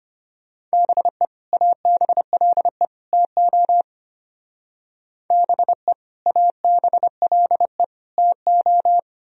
Morse code, heard as a keyed tone, about 30 wpm, 700 Hz. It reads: BE ABLE TO BE ABLE TO
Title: Speed-Racing - Sets of 3 Words 25wpm